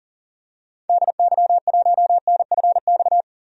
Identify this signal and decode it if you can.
Morse code, heard as a keyed tone, about 40 wpm, 700 Hz. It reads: DY1NFX
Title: DXCC-100 Call Signs - 6 Characters - 40wpm